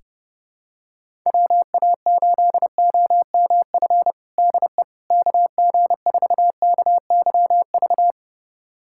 Morse code, heard as a keyed tone, about 32 wpm, 700 Hz. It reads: WA8OMF DE KG4KYV
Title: USA Callsign DE Callsign - 6 Characters - 30wpm